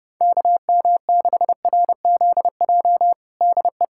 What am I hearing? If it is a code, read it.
Morse code, heard as a keyed tone, about 30 wpm, 700 Hz. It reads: KM6RZJ DE
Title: USA Callsign DE Callsign - 6 Characters - 30wpm